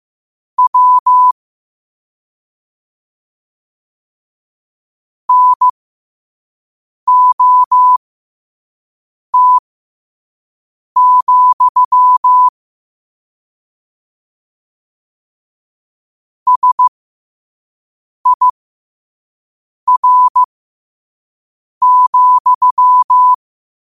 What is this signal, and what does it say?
Morse code, heard as a keyed tone, about 15 wpm, spaced as for 5 wpm, 1000 Hz. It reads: W NOT, SIR,